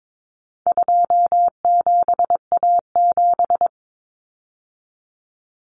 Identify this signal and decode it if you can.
Morse code, heard as a keyed tone, about 22 wpm, 700 Hz. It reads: 27A7
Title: Common Cut Numbers - 4-Digit 22wpm